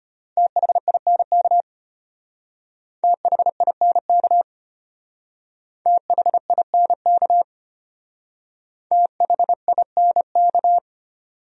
Morse code, heard as a keyed tone, about 32 wpm, 700 Hz. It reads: THINK THINK THINK THINK